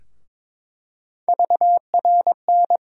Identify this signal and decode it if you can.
Morse code, heard as a keyed tone, about 22 wpm, 700 Hz. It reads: VRN
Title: USA Call Sign Postfixes - 3 Letters 22wpm